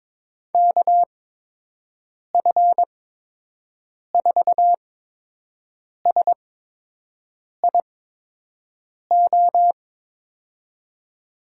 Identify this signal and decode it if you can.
Morse code, heard as a keyed tone, about 22 wpm, spaced as for 6 wpm, 700 Hz. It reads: KF4SIO
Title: USA Call Signs - 6 Characters - 8x Character Spacing - 22wpm